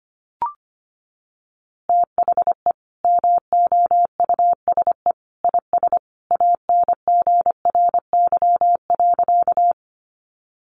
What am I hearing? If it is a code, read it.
Morse code, heard as a keyed tone, about 25 wpm, 700 Hz. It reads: THE MOUSE IS ANGRY.
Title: Easy Phrases 25wpm